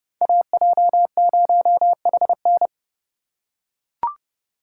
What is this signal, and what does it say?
Morse code, heard as a keyed tone, about 30 wpm, 700 Hz. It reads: AJ0HN